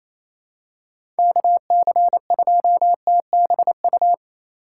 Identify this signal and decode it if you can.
Morse code, heard as a keyed tone, about 28 wpm, 700 Hz. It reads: KC2TBU
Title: USA Call Signs - 2x Character Spacing - 28wpm